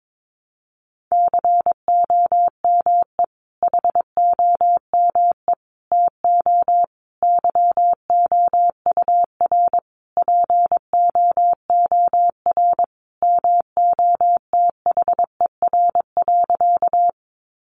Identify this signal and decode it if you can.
Morse code, heard as a keyed tone, about 22 wpm, 700 Hz. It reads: COME HOME TO YOUR POOR MOTHER.
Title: Sentences from Top 1000 Words 22wpm